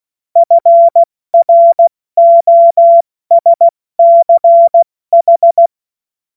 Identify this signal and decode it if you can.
Morse code, heard as a keyed tone, about 16 wpm, 680 Hz. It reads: FROSCH
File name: Morsekurs / Witze / Witz 08. November 2022 70 F80